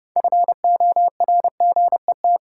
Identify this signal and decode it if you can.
Morse code, heard as a keyed tone, about 30 wpm, 700 Hz. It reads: FORGET
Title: Sentences from Top 1200 Words - New Word Drill Alt - 30wpm